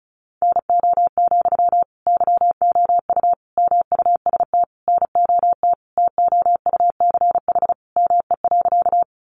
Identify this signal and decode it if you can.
Morse code, heard as a keyed tone, about 34 wpm, 700 Hz. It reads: NO, YOU MUST NOT TOUCH ME.
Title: Sentences from Top 1200 Words 35wpm